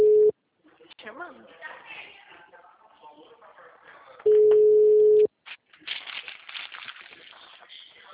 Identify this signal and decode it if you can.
Ringback tone (Europe, 425 Hz), with speech.